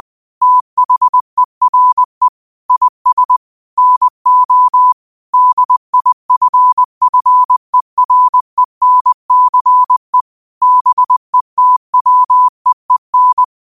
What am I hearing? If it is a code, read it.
Morse code, heard as a keyed tone, about 20 wpm, 1000 Hz. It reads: THERE IS NO DIFFERENCE BETWEEN
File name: 20wpm